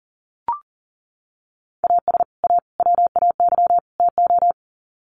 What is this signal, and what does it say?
Morse code, heard as a keyed tone, about 40 wpm, 700 Hz. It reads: AS A WAY TO
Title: Sets of 4 Words 40wpm